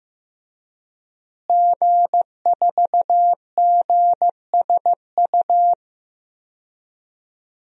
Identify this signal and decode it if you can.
Morse code, heard as a keyed tone, about 15 wpm, 700 Hz. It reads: G4GSU
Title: DXCC-100 Call Signs - 5 Characters - Repeated 3x - 15wpm